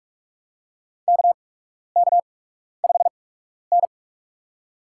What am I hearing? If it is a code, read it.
Morse code, heard as a keyed tone, about 43 wpm, spaced as for 12 wpm, 700 Hz. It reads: KK5N
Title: USA Call Signs - 4 Characters - 8x Character Spacing - 45wpm